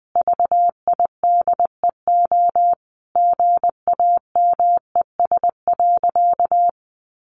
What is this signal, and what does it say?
Morse code, heard as a keyed tone, about 20 wpm, 700 Hz. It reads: VIDEO GAMES.
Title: Additional Easy Phrases 20wpm